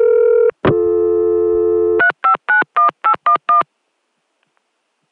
DTMF, heard as telephone touch tones, tones of about 100 ms. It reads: 3591811